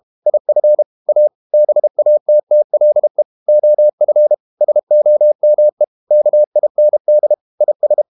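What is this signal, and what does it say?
Morse code, heard as a keyed tone, about 32 wpm, 600 Hz. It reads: IF A BATTLE OF SOME KIND IS
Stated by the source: Morse code practice at 32 wpm from CW med Gnister, reading Ken Follett's novel Never in English.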